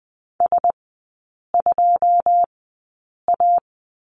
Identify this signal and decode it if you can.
Morse code, heard as a keyed tone, about 20 wpm, spaced as for 8 wpm, 700 Hz. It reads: S2A
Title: 081 Lesson 14e - Number 2 - Three Characters 20wpm